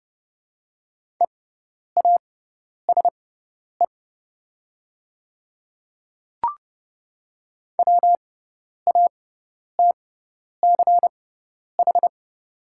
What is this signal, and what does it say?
Morse code, heard as a keyed tone, about 32 wpm, spaced as for 10 wpm, 700 Hz. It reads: EASE WATCH
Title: 6x Character Spacing - Top 300 Words - 30wpm